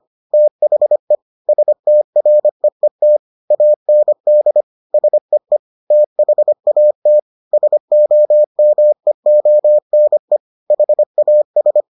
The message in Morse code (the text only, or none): THE STREET AND SEE THAT SOMEONE HAS